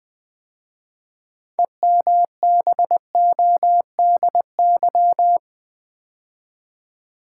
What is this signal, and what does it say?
Morse code, heard as a keyed tone, about 20 wpm, 700 Hz. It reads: EMBODY
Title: 6 Letter Words - First 3 Letters Given - 20wpm